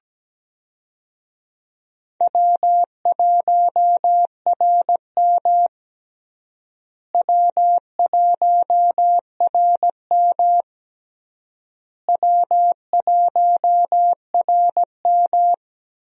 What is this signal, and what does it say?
Morse code, heard as a keyed tone, about 17 wpm, 700 Hz. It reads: W1RM W1RM W1RM
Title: USA Call Signs - 4 Characters - Repeated 3x - 17wpm